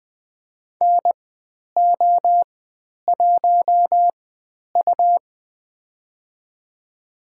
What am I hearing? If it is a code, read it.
Morse code, heard as a keyed tone, about 20 wpm, spaced as for 10 wpm, 700 Hz. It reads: NO1U
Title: 102 Lesson 17g - Letter u - Call Signs 20wpm